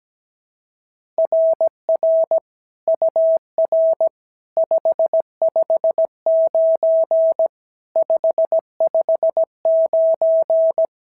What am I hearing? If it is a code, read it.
Morse code, heard as a keyed tone, about 16 wpm, 650 Hz. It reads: RR UR 559 559
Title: SOTA QSO Practice 17wpm